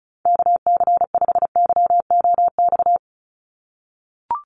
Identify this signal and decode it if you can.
Morse code, heard as a keyed tone, about 34 wpm, 700 Hz. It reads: KC5YOX
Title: USA Call Signs 35wpm